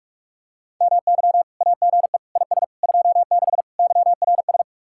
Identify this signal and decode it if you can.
Morse code, heard as a keyed tone, about 45 wpm, 700 Hz. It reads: MY AGE IS 26 YRS